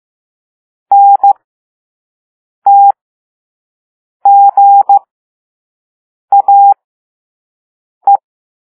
Morse code, heard as a keyed tone, about 15 wpm, spaced as for 5 wpm, 800 Hz. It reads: NTGAE
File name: Morse / Level6-All-A